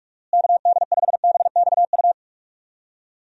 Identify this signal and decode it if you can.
Morse code, heard as a keyed tone, about 41 wpm, 700 Hz. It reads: KD5BXU
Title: USA Call Signs - 6 Characters - Repeated 3x - 45wpm